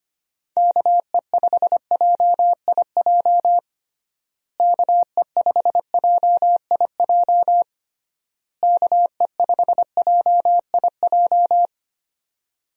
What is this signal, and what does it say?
Morse code, heard as a keyed tone, about 25 wpm, 700 Hz. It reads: KE5JIJ KE5JIJ KE5JIJ
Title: USA Call Signs - 6 Characters - Repeated 3x - 25wpm